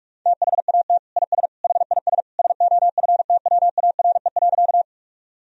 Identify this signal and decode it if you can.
Morse code, heard as a keyed tone, about 46 wpm, 700 Hz. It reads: THAT IS HIS SOFTWARE.
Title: Sentences from Top 1000 Words - Review Part 3 - 45wpm